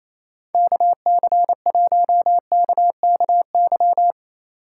Morse code, heard as a keyed tone, about 35 wpm, 700 Hz. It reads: KC1KKY
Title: Speed Racing - USA 2x3 Call Signs 28wpm